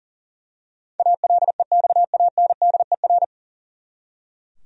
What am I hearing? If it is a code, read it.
Morse code, heard as a keyed tone, about 40 wpm, 700 Hz. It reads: ALEXANDER
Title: USA Common Names - 40wpm